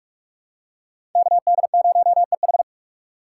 Morse code, heard as a keyed tone, about 45 wpm, 690 Hz. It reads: KD0EH
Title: USA Call Signs - 5 Characters - 2x Character Spacing - 45wpm